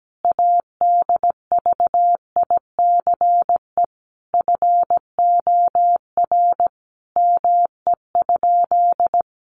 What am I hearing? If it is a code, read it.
Morse code, heard as a keyed tone, about 17 wpm, 700 Hz. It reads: ADVICE FOR ME?
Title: Sentences from Top 1100 Words 17wpm